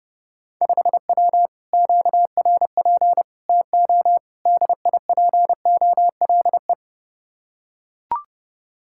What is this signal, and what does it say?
Morse code, heard as a keyed tone, about 32 wpm, 700 Hz. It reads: 5W QRP TO DIPOLE